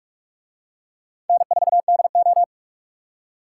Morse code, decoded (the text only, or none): N4DO